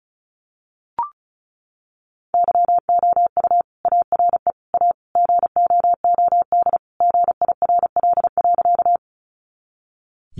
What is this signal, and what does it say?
Morse code, heard as a keyed tone, about 34 wpm, 700 Hz. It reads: YOU ARE A GOOD GIRL.
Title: Sentences from Top 400 Words 35wpm